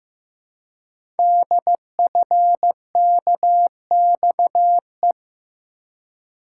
Morse code, heard as a keyed tone, about 15 wpm, 700 Hz. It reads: DFKXE
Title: Code Groups - 5 Letters - 15wpm